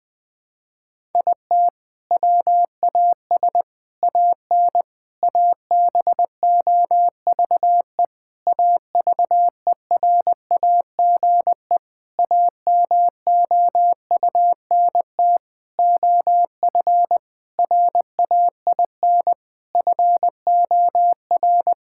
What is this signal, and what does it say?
Morse code, heard as a keyed tone, about 21 wpm, 700 Hz. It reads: IT WAS AN ABOVE AVERAGE AMOUNT OF RAIN FOR
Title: Sentences from Top 1000 Words - Review Part 6 - 20wpm